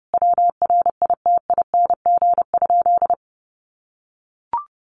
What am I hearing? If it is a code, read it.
Morse code, heard as a keyed tone, about 30 wpm, 700 Hz. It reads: WRITING?
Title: Sentences from Top 1200 Words 30wpm